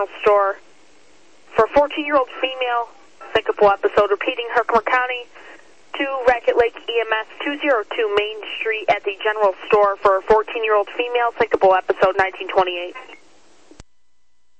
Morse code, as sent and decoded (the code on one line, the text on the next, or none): none